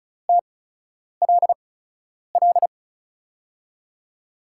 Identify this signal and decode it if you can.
Morse code, heard as a keyed tone, about 35 wpm, spaced as for 10 wpm, 700 Hz. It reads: TLL
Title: USA Call Signs - 6 Characters - 8x Character Spacing - 35wpm